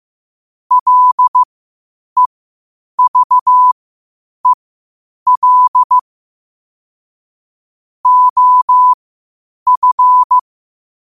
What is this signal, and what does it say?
Morse code, heard as a keyed tone, about 15 wpm, spaced as for 8 wpm, 1000 Hz. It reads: LEVEL OF
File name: qotdfiles / 8wpm